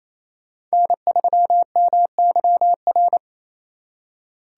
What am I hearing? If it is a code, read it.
Morse code, heard as a keyed tone, about 35 wpm, 700 Hz. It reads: N3MYR